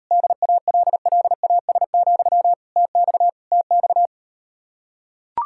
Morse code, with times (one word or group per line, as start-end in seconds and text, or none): 0.1-2.5 DALLAS,
2.8-3.3 TX
3.5-4.1 TX